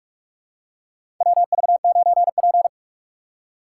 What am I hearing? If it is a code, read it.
Morse code, heard as a keyed tone, about 44 wpm, 700 Hz. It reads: WV9P
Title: USA Call Signs - 4 Characters - 6x Character Spacing - 45wpm